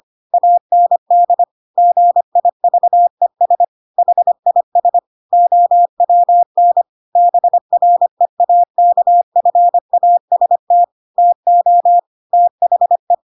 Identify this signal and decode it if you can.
Morse code, heard as a keyed tone, about 25 wpm, 700 Hz. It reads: AND GIVES HIS OWN BREAKFAST TO THE
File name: Pinocchio / Pinocchio ch7 25 25 ik1hge